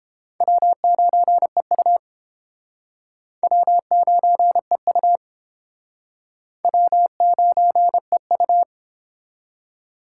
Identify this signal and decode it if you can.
Morse code, heard as a keyed tone, about 30 wpm, 700 Hz. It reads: W9EU W9EU W9EU